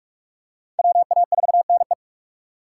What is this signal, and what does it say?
Morse code, heard as a keyed tone, about 40 wpm, 700 Hz. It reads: WA4NE